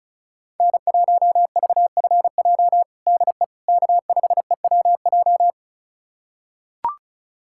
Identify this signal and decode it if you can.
Morse code, heard as a keyed tone, about 36 wpm, 700 Hz. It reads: N1VFJ DE K5EWJ
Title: USA Callsign DE Callsign - 5 Characters - 35wpm